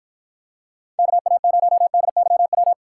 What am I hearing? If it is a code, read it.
Morse code, heard as a keyed tone, about 53 wpm, 700 Hz. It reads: KA0DYW